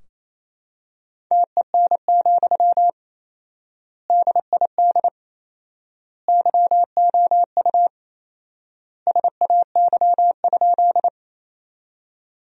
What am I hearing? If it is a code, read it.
Morse code, heard as a keyed tone, about 27 wpm, 700 Hz. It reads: TEN, DID YOU SAY?
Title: 4x Word-Spacing - Sentences from Top 400 Words - 28wpm